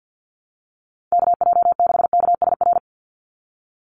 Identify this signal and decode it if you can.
Morse code, heard as a keyed tone, about 50 wpm, 700 Hz. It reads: KJ6KSR
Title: USA Call Signs 50wpm